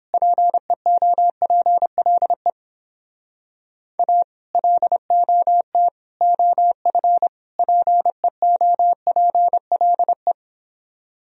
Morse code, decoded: PEOPLE A LOT OF PEOPLE

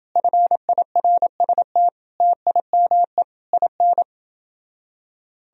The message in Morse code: FIRST TIME IN

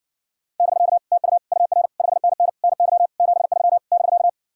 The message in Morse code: <BK> TU UR 5NN NY 73 <BK>

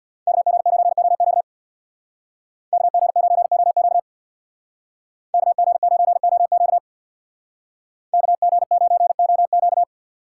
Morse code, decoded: KG9OQ KG9OQ KG9OQ KG9OQ

